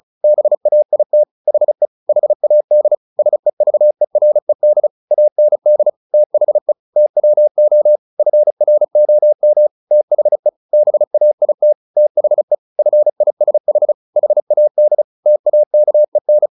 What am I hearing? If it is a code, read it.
Morse code, heard as a keyed tone, about 34 wpm, 600 Hz. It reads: BAIT HE HAD SEVERED AND THE TWO FROM THE BAIT THE FISH HAD TAKEN
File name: the old man and the sea / the old man and the sea 1 35wpm